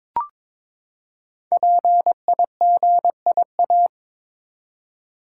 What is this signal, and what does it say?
Morse code, heard as a keyed tone, about 22 wpm, 700 Hz. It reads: PIGIA
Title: Code Groups - 5 Letters - 22wpm